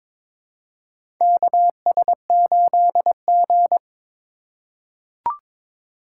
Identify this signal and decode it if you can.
Morse code, heard as a keyed tone, about 22 wpm, 700 Hz. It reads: KS8G
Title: USA Call Signs - 4 Characters - 4x Character Spacing - 22wpm